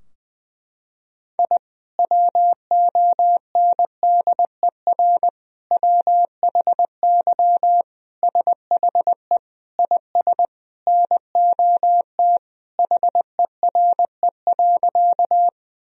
Morse code, heard as a keyed tone, about 20 wpm, 700 Hz. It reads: I WONDER WHY SHE IS NOT HERE.
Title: Sentences from Top 1000 Words - Review Part 3 - 20wpm